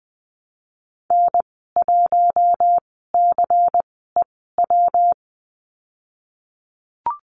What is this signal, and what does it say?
Morse code, heard as a keyed tone, about 20 wpm, spaced as for 15 wpm, 700 Hz. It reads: N1CEW
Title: 213 Lesson 34g - Number 8 - Call Signs 20wpm